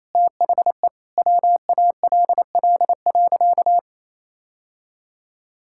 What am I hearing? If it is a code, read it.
Morse code, heard as a keyed tone, about 28 wpm, 700 Hz. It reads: THE WALL.
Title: Sentences from Top 1600 Words 28wpm